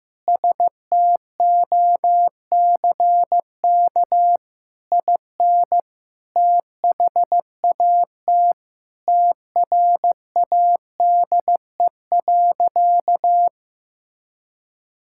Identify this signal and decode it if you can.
Morse code, heard as a keyed tone, about 16 wpm, 700 Hz. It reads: STOCK IN THAT TRADE.
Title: Sentences from Top 1000 Words - Review Part 5 - 15wpm